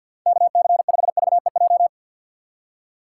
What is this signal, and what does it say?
Morse code, heard as a keyed tone, about 48 wpm, 700 Hz. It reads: KC5VEJ
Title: USA Call Signs - 6 Characters - 6x Character Spacing - 50wpm